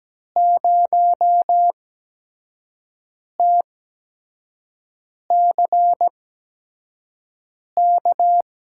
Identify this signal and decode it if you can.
Morse code, heard as a keyed tone, about 17 wpm, spaced as for 5 wpm, 700 Hz. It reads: 0TCK
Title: USA Call Signs - 6 Characters - 8x Character Spacing - 17wpm